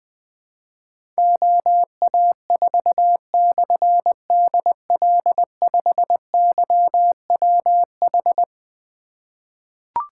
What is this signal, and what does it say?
Morse code, heard as a keyed tone, about 20 wpm, 700 Hz. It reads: OA4/DL5YWH
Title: Difficult Call Signs 20wpm